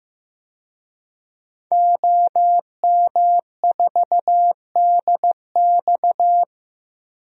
Morse code, heard as a keyed tone, about 15 wpm, 700 Hz. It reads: OM4DX
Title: DXCC-100 Call Signs - 5 Characters - 15wpm